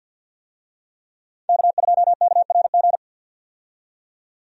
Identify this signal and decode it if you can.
Morse code, heard as a keyed tone, about 50 wpm, 700 Hz. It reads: K2KRG
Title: USA Call Signs - 5 Characters - 50wpm